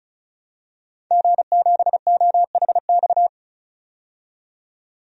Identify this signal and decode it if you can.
Morse code, heard as a keyed tone, about 36 wpm, 700 Hz. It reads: G7OHX
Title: DXCC-100 Call Signs - 5 Characters - 35wpm